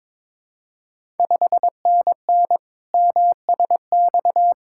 Morse code, heard as a keyed tone, about 22 wpm, 700 Hz. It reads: 5NN MSX
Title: Ontario QSO Party Exchange 22wpm